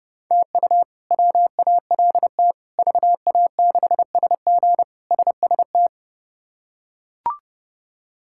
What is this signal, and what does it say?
Morse code, heard as a keyed tone, about 30 wpm, 700 Hz. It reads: TU WALT VA6SG SST
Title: K1USN SST QSO Practice - 30wpm